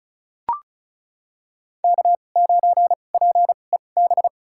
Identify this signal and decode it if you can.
Morse code, heard as a keyed tone, about 34 wpm, spaced as for 25 wpm, 700 Hz. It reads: K9PEB
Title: USA Call Signs - 5 Characters - 2x Character Spacing - 35wpm